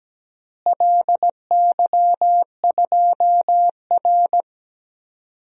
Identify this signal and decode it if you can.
Morse code, heard as a keyed tone, about 17 wpm, 700 Hz. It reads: LY2R
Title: DXCC-100 Call Signs - 4 Characters - Repeated 2x - 17wpm